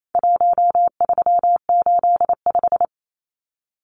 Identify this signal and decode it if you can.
Morse code, heard as a keyed tone, about 28 wpm, 700 Hz. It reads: 1385
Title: Numbers - 4-Digit 28wpm